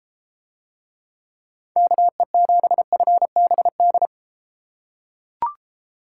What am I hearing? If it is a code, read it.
Morse code, heard as a keyed tone, about 32 wpm, 700 Hz. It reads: KE7FBD